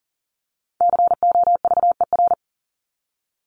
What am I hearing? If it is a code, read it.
Morse code, heard as a keyed tone, about 40 wpm, 700 Hz. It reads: COVER
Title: Top 300 Words 40wpm